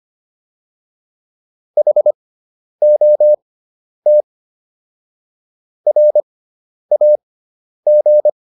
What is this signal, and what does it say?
Morse code, heard as a keyed tone, about 25 wpm, spaced as for 10 wpm, 600 Hz. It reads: HOT RAG